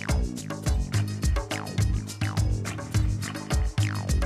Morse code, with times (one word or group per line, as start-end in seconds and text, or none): none